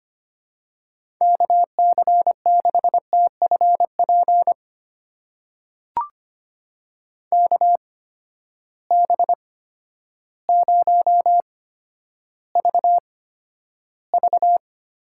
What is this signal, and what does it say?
Morse code, heard as a keyed tone, about 25 wpm, spaced as for 11 wpm, 700 Hz. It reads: KC6TFP KB0VV
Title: USA Call Signs - 6 Characters - 8x Character Spacing - 25wpm